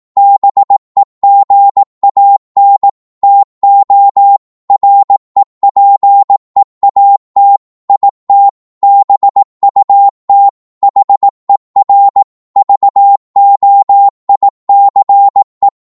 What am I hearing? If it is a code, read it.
Morse code, heard as a keyed tone, about 18 wpm, 800 Hz. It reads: BEGAN TO REPEAT IT BUT HER VOICE